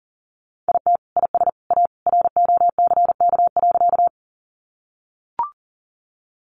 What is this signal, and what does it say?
Morse code, heard as a keyed tone, about 40 wpm, 700 Hz. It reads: IT IS A ROCK.
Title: Sentences from Top 400 Words 40wpm